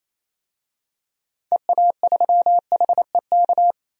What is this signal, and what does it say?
Morse code, heard as a keyed tone, about 28 wpm, 700 Hz. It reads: EA3HEK
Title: DXCC-100 Call Signs - 6 Characters - 28wpm